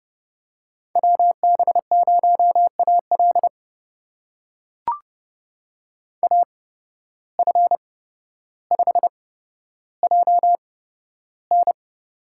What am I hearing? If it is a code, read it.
Morse code, heard as a keyed tone, about 30 wpm, spaced as for 13 wpm, 700 Hz. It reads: WB0AL AF5JN
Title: USA Call Signs - 5 Characters - 8x Character Spacing - 30wpm